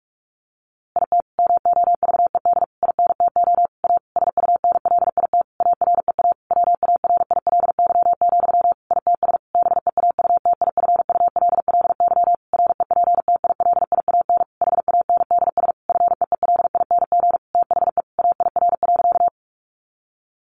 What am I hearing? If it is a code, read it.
Morse code, heard as a keyed tone, about 44 wpm, 700 Hz. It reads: IT MOVED INTO A SUNLIT AREA WARILY, ITS BEAUTIFULLY REPTILIAN HANDS FEELING THE AIR.
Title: A Sound Of Thunder 45wpm